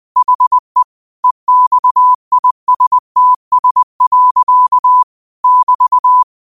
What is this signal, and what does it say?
Morse code, heard as a keyed tone, about 20 wpm, 1000 Hz. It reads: HE EXISTS. <BT>
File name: qotdfiles / 20wpm